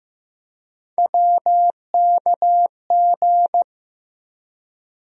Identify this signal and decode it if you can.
Morse code, heard as a keyed tone, about 15 wpm, 700 Hz. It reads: WKG
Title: USA Call Sign Postfixes - 3 Letters 15wpm